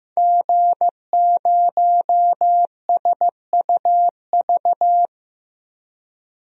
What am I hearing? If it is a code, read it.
Morse code, heard as a keyed tone, about 15 wpm, 700 Hz. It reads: G0SUV
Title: DXCC-100 Call Signs - 5 Characters - Repeated 2x - 15wpm